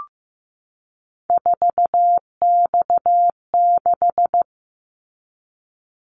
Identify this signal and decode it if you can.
Morse code, heard as a keyed tone, about 15 wpm, 700 Hz. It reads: 4X6
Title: Callsign Prefixes for Top 50 DXCC - 15wpm